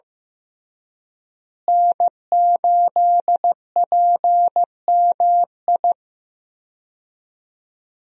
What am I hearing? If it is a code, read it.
Morse code, heard as a keyed tone, about 15 wpm, 700 Hz. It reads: N8PMI
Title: USA Call Signs - 5 Characters - Repeated 3x - 15wpm